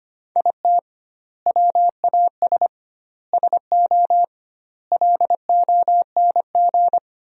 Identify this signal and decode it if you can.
Morse code, heard as a keyed tone, about 25 wpm, 700 Hz. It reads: IT WAS SO LONG
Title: 2x Word-Spacing - Sentences from Top 500 Words - 25wpm